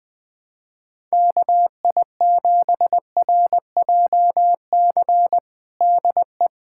Morse code, heard as a keyed tone, about 20 wpm, 700 Hz. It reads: KI7RJC DE
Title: USA Callsign DE Callsign - 6 Characters - 20wpm